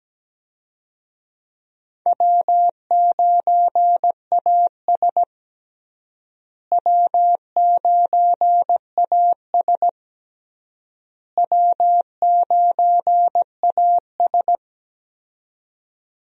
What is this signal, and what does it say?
Morse code, heard as a keyed tone, about 17 wpm, 700 Hz. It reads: W9AS W9AS W9AS
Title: USA Call Signs - 4 Characters - Repeated 3x - 17wpm